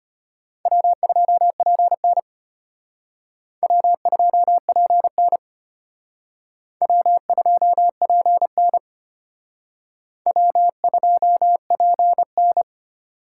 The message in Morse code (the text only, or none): W2PN W2PN W2PN W2PN